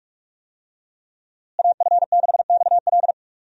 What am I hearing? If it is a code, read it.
Morse code, heard as a keyed tone, about 46 wpm, 700 Hz. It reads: AF6XL
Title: USA Call Signs - 5 Characters - Repeated 3x - 45wpm